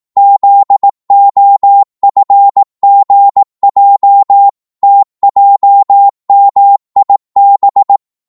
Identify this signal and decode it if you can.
Morse code, heard as a keyed tone, about 18 wpm, 800 Hz. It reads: ZOFGJ TJMIB
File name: Morse / ltr-18wpm